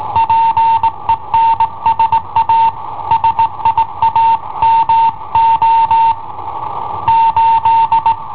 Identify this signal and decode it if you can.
Morse code, heard as a keyed tone, about 18 wpm, 910 Hz. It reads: PENSA SIAMO 8
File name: i7ohp / cassetta i7ohp B